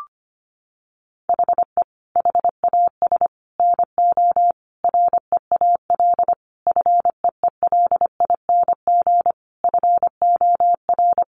Morse code, heard as a keyed tone, about 25 wpm, 700 Hz. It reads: HE HAS NO REAL FEELING FOR
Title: Sentences from Top 1200 Words 25wpm